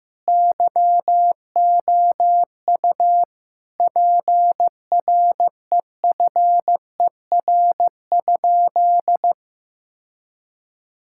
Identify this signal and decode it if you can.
Morse code, heard as a keyed tone, about 15 wpm, 700 Hz. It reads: YOU PREFER?
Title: Sentences from Top 1600 Words 15wpm